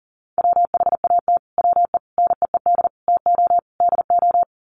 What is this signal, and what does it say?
Morse code, heard as a keyed tone, about 40 wpm, 700 Hz. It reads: WHAT WE NEED TO DO
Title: Sets of 5 Words Encore 40wpm